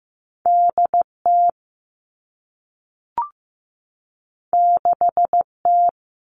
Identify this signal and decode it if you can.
Morse code, heard as a keyed tone, about 15 wpm, 700 Hz. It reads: D T 6 T
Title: Two Challenging Letter-Number Combinations 15wpm